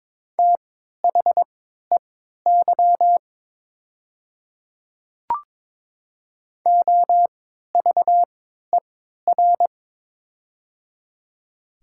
Morse code, heard as a keyed tone, about 22 wpm, spaced as for 12 wpm, 700 Hz. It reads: THEY OVER
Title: 3x Character Spacing - Top 100 Words - 22wpm